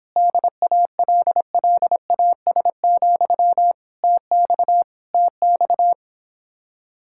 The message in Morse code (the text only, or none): DALLAS, TX TX